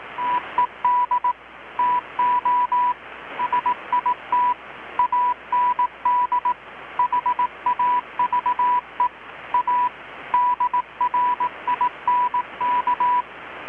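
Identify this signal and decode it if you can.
Morse code, heard as a keyed tone, about 18 wpm, 980 Hz. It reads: TED TO SIT AND HAVE A DRINK
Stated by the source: Morse from KPH's Night of Nights XX maritime radiotelegraphy event, received in Chiapas, Mexico about 3,800 km away on a Kenwood R-600 with a grounded 10 m long-wire antenna.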